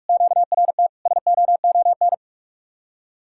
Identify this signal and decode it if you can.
Morse code, heard as a keyed tone, about 49 wpm, 700 Hz. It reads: QRT SOON